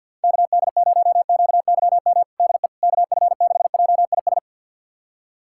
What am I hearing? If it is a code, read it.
Morse code, heard as a keyed tone, about 49 wpm, 700 Hz. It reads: KD0QYM DE KF6JIS
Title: USA Callsign DE Callsign - 6 Characters - 50wpm